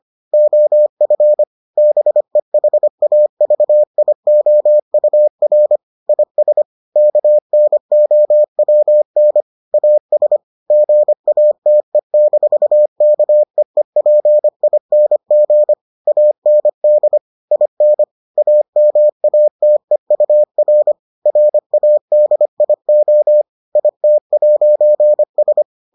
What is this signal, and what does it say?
Morse code, heard as a keyed tone, about 25 wpm, 600 Hz. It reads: OF BEHAVIOUR IS KNOWN AS GATE-KEEPING AND IN AMATEUR RADIO IT'S